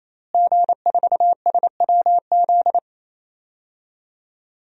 Morse code, heard as a keyed tone, about 28 wpm, 700 Hz. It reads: G4SWZ
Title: DXCC-100 Call Signs - 5 Characters - Repeated 3x - 28wpm